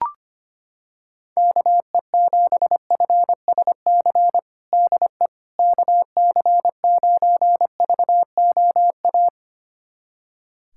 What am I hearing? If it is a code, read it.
Morse code, heard as a keyed tone, about 25 wpm, 700 Hz. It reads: KE7FSC DE KC9VOA
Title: USA Callsign DE Callsign - 6 Characters - 25wpm